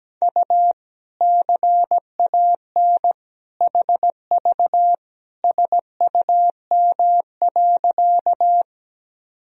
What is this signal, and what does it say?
Morse code, heard as a keyed tone, about 17 wpm, 700 Hz. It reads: U CAN HV SUM.
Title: Sentences from Top 100 Words and Abbreviations 17wpm